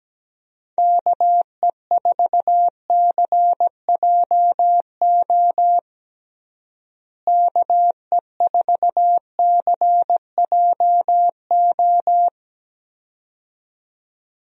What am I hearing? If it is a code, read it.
Morse code, heard as a keyed tone, about 17 wpm, 700 Hz. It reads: KE4CJO KE4CJO
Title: USA Call Signs - 6 Characters - Repeated 2x - 17wpm